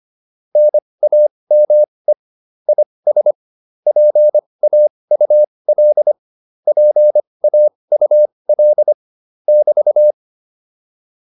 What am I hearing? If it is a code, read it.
Morse code, heard as a keyed tone, about 25 wpm, spaced as for 20 wpm, 600 Hz. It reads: NAME IS PAUL PAUL <BT>